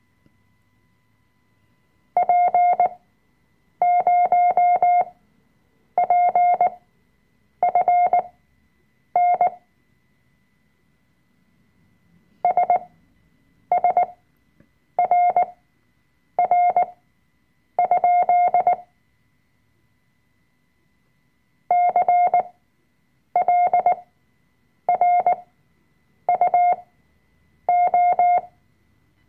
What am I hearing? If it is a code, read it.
Morse code, heard as a keyed tone, about 19 wpm, spaced as for 7 wpm, 710 Hz. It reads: P0PFN SSRR? CLRUO
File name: Workshop Dateien / CW-Online-Kurs / Lektion 005